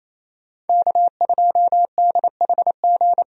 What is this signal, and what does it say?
Morse code, heard as a keyed tone, about 28 wpm, 700 Hz. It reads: K2DHG
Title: USA Call Signs - 5 Characters - 28wpm